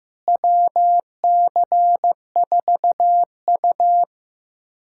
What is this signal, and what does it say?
Morse code, heard as a keyed tone, about 15 wpm, 700 Hz. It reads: WC4U
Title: USA Callsign DE Callsign - 4 Characters - 15wpm